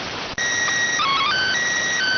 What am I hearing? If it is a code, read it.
SSTV (slow-scan television), PD 290.